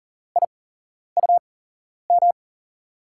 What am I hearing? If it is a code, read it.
Morse code, heard as a keyed tone, about 40 wpm, spaced as for 11 wpm, 700 Hz. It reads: IUM
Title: USA Call Signs - 6 Characters - 8x Character Spacing - 40wpm